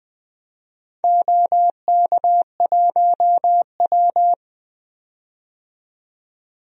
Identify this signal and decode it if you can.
Morse code, heard as a keyed tone, about 20 wpm, 700 Hz. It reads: OK1W
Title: DXCC-100 Call Signs - 4 Characters - 20wpm